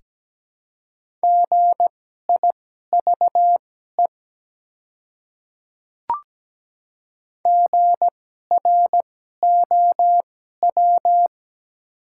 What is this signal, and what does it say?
Morse code, heard as a keyed tone, about 17 wpm, spaced as for 12 wpm, 700 Hz. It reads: GIVE GROW
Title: 2x Character Spacing - Top 300 Words - 17wpm